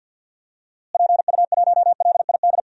Morse code, heard as a keyed tone, about 51 wpm, 700 Hz. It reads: PU1LID
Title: DXCC-100 Call Signs - 6 Characters - Repeated 3x - 50wpm